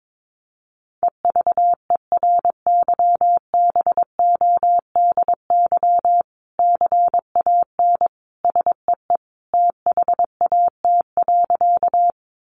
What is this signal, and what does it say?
Morse code, heard as a keyed tone, about 22 wpm, 700 Hz. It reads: EVERYBODY CAN SEE THAT.
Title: Sentences from Top 1400 Words 22wpm